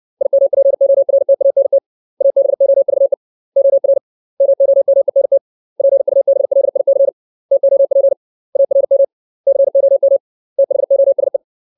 Morse code, heard as a keyed tone, about 60 wpm, 550 Hz. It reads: IMPORTANT ABOVE OR MOMENT PUBLIC TOP ARM COM THOSE
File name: 632 words / 632 Words 60WPM